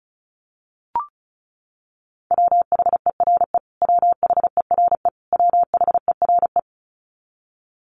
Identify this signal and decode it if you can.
Morse code, heard as a keyed tone, about 34 wpm, 700 Hz. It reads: WHERE WHERE WHERE
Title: Top 500 Words Repeated 35wpm